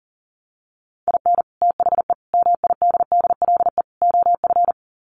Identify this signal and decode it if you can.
Morse code, heard as a keyed tone, about 40 wpm, 700 Hz. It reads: IN THE MIDDLE OF
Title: Sets of 4 Words 40wpm